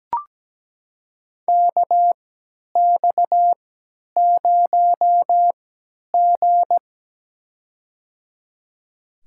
Morse code, heard as a keyed tone, about 17 wpm, spaced as for 10 wpm, 700 Hz. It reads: KX0G
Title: USA Call Signs - 4 Characters - 3x Character Spacing - 17wpm